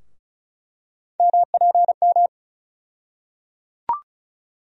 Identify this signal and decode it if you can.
Morse code, heard as a keyed tone, about 34 wpm, 700 Hz. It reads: MPM
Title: USA Call Sign Postfixes - 3 Letters 35wpm